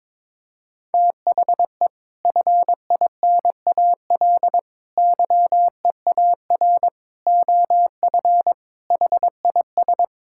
Morse code, heard as a keyed tone, about 22 wpm, 700 Hz. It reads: THE FINAL YEAR OF HIS
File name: practice / 22wpm / Sets of 6 Words - 22wpm